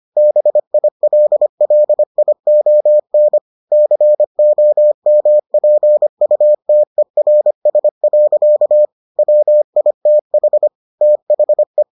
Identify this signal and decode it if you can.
Morse code, heard as a keyed tone, about 25 wpm, 600 Hz. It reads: BILLION COMPUTERS. WITH THE